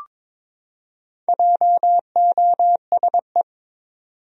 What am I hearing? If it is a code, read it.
Morse code, heard as a keyed tone, about 22 wpm, 700 Hz. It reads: JOSE
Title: USA Common Names - 22wpm